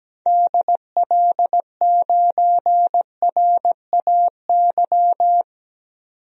Morse code, heard as a keyed tone, about 17 wpm, 700 Hz. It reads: DL9RAY
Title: DXCC-100 Call Signs - 6 Characters - 17wpm